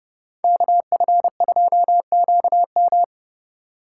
Morse code, heard as a keyed tone, about 31 wpm, 700 Hz. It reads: KF2QM
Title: USA Call Signs - 5 Characters - 3x Character Spacing - 30wpm